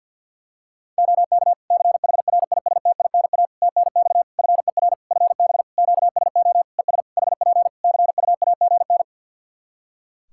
Morse code, heard as a keyed tone, about 50 wpm, 700 Hz. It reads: OK CHRISTINA TNX FER FB QSO ES HP CUAGN
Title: Practice QSO - Exchange 4 - Ending - 50wpm